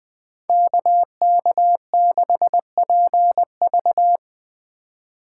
Morse code, heard as a keyed tone, about 20 wpm, 700 Hz. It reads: KK6PV
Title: Speed Racing - USA 2x2 Call Signs 17wpm